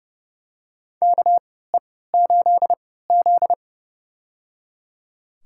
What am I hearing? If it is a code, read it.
Morse code, heard as a keyed tone, about 30 wpm, spaced as for 17 wpm, 700 Hz. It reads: KE8Z
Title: USA Call Signs - 3x Character Spacing - 30wpm